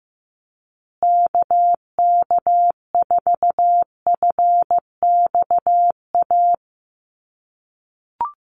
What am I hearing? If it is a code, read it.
Morse code, heard as a keyed tone, about 15 wpm, 700 Hz. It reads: KK4FXA
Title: USA Call Signs Repeated 2x 15wpm